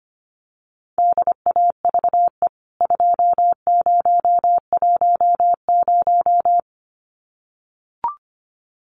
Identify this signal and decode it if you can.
Morse code, heard as a keyed tone, about 25 wpm, 700 Hz. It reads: DAVE 2010
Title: CWT Exchange 25wpm